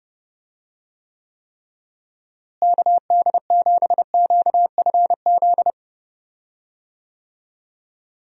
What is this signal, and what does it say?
Morse code, heard as a keyed tone, about 30 wpm, 700 Hz. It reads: KD7QFZ
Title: USA Call Signs - 6 Characters - Repeated 3x - 30wpm